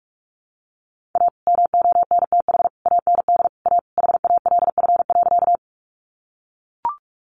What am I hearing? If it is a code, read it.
Morse code, heard as a keyed tone, about 44 wpm, 700 Hz. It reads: A MONTH AND A HALF.
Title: Sentences from Top 800 Words 45wpm